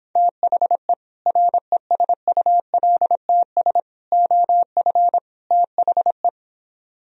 Morse code, heard as a keyed tone, about 26 wpm, 700 Hz. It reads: THE RESULTS OF THE